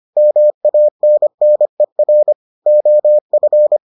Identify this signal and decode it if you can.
Morse code, heard as a keyed tone, about 25 wpm, 600 Hz. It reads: MANNER OF